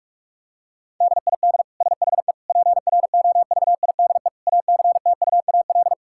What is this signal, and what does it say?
Morse code, heard as a keyed tone, about 41 wpm, 700 Hz. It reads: DID SHE PROVIDE ACTUAL
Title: Sentences from Top 1000 Words - Review Part 5 - 45wpm